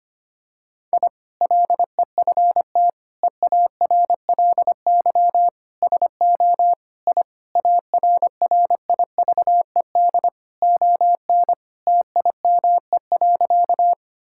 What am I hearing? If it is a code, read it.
Morse code, heard as a keyed tone, about 25 wpm, 700 Hz. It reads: I LEFT EARLY SO I ARRIVED ON TIME.
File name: practice / 25wpm / Sentences with 2-Word Phrases - 25wpm